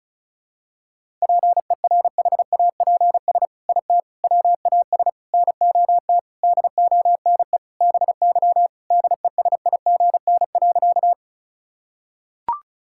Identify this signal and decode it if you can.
Morse code, heard as a keyed tone, about 36 wpm, 700 Hz. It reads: PERHAPS IT WAS NOT DONE BY DESIGN.
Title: Sentences from Top 1000 Words - Review Part 5 - 35wpm